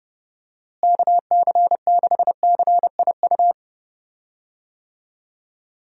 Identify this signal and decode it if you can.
Morse code, heard as a keyed tone, about 30 wpm, 700 Hz. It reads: KC6CIU
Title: USA Call Signs - 6 Characters - Repeated 3x - 30wpm